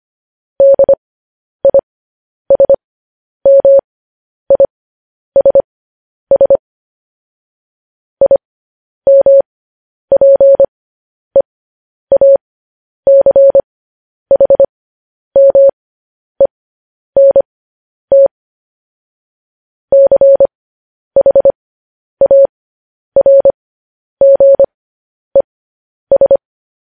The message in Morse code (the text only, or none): DISMISS IMPEACHMENT CHARGES